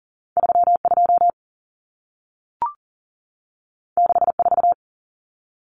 Morse code, heard as a keyed tone, about 40 wpm, 700 Hz. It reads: 32 64
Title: Numbers - 2-Digit 40wpm